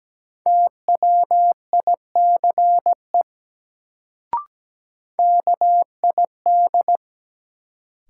Morse code, heard as a keyed tone, about 17 wpm, 700 Hz. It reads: TWICE KID
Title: Sentences from Top 1600 Words - New Word Drill Alt - 17wpm